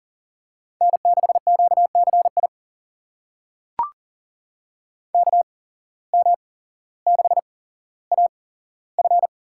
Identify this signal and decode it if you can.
Morse code, heard as a keyed tone, about 40 wpm, spaced as for 17 wpm, 700 Hz. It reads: N6QCI KM6AF
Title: USA Call Signs - 5 Characters - 8x Character Spacing - 40wpm